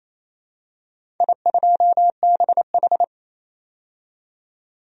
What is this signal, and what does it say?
Morse code, heard as a keyed tone, about 28 wpm, 700 Hz. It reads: I2BH
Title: DXCC-100 Call Signs - 4 Characters - 28wpm